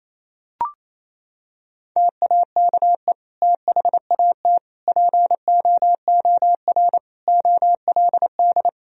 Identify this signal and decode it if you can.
Morse code, heard as a keyed tone, about 27 wpm, 700 Hz. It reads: TAKE THAT POOR OLD
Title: Sentences from Top 1000 Words - Review Part 2 - 28wpm